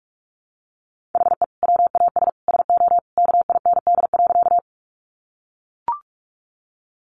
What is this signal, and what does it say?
Morse code, heard as a keyed tone, about 44 wpm, 700 Hz. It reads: HE WAS SO KIND.
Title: Sentences from Top 200 Words 45wpm